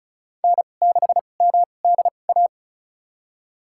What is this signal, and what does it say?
Morse code, heard as a keyed tone, about 36 wpm, spaced as for 25 wpm, 700 Hz. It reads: N6MDA
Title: USA Call Signs - 5 Characters - 2x Character Spacing - 35wpm